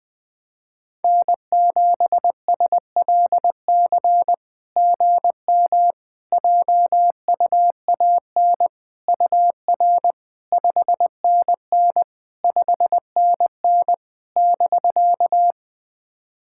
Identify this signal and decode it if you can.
Morse code, heard as a keyed tone, about 20 wpm, 700 Hz. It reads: N7SLC GM JUAN UR 5NN 5NN <BK>
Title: SOTA QSO Practice 20wpm